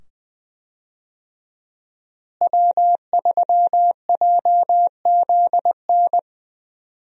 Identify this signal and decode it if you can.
Morse code, heard as a keyed tone, about 20 wpm, 700 Hz. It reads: W3JZN
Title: USA Call Signs - 5 Characters - Repeated 2x - 20wpm